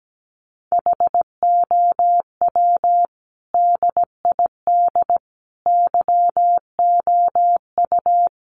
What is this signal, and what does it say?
Morse code, heard as a keyed tone, about 17 wpm, 700 Hz. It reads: HOW DID YOU
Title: Sentences from Top 100 Words 17wpm